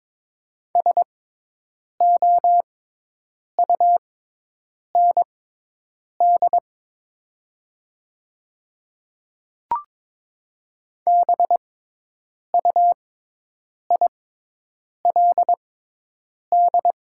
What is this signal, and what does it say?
Morse code, heard as a keyed tone, about 22 wpm, spaced as for 8 wpm, 700 Hz. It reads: SOUND BUILD
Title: 6x Character Spacing - Top 300 Words - 22wpm